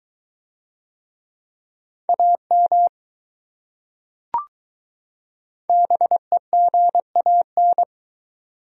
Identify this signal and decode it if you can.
Morse code, heard as a keyed tone, about 22 wpm, 700 Hz. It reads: AM BEGAN